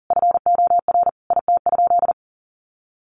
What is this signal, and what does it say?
Morse code, heard as a keyed tone, about 40 wpm, 700 Hz. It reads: FOR IT?
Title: Sentences from Top 900 Words 40wpm